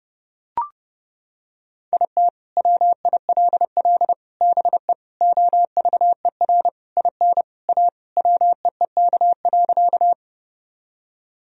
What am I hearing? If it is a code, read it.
Morse code, heard as a keyed tone, about 30 wpm, 700 Hz. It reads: IT WILL BE OVER IN A WEEK.
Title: Sentences from Top 1000 Words - Review Part 2 - 30wpm